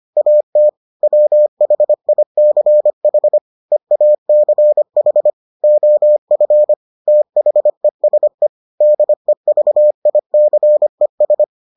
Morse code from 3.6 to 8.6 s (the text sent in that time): EACH OF THESE